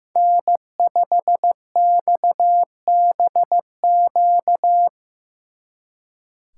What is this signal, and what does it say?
Morse code, heard as a keyed tone, about 18 wpm, 700 Hz. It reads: N5XBQ
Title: Speed Racing - USA Call Signs 15wpm